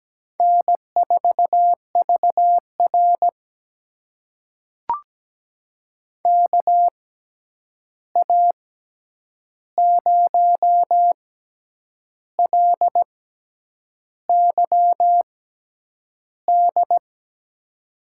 Morse code, heard as a keyed tone, about 17 wpm, spaced as for 8 wpm, 700 Hz. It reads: N4VR KA0LYD
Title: USA Call Signs - 6x Character Spacing - 17wpm